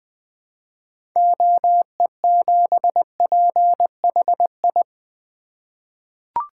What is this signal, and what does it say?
Morse code, heard as a keyed tone, about 20 wpm, 700 Hz. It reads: OE7PHI
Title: 2023 SOTA Chaser Call Signs - 20wpm